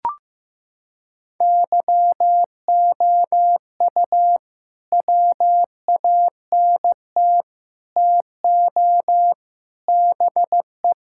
Morse code, heard as a keyed tone, about 15 wpm, 700 Hz. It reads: YOU WANT TO BE
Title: Sentences from Top 1600 Words 15wpm